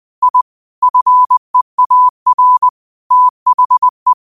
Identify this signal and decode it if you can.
Morse code, heard as a keyed tone, about 20 wpm, 1000 Hz. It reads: I FEAR THE
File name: qotdfiles / 20wpm